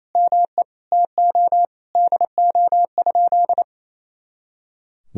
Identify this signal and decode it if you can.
Morse code, heard as a keyed tone, about 28 wpm, 700 Hz. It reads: ME TO DO?
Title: Sentences from Top 200 Words and Abbreviations 28wpm